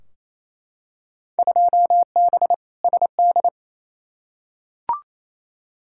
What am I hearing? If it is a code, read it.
Morse code, heard as a keyed tone, about 28 wpm, 700 Hz. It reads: 2B SD
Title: Field Day Exchanges Encore - 28wpm